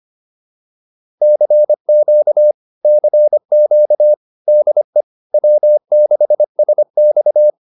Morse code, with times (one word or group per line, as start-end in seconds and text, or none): 1.2-2.5 CQ
2.8-4.1 CQ
4.5-5.0 DE
5.3-7.5 W6SX